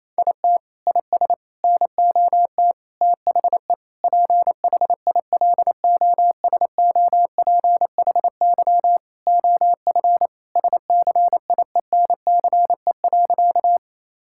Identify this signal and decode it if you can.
Morse code, heard as a keyed tone, about 28 wpm, 700 Hz. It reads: IT IS NOT THE PHILOSOPHY OF SCIENCE.
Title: Sentences from Top 1000 Words - Review Part 6 - 28wpm